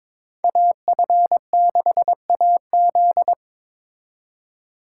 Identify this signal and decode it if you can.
Morse code, heard as a keyed tone, about 26 wpm, 700 Hz. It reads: AF6AZ
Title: Speed Racing - USA 2x2 Call Signs 22wpm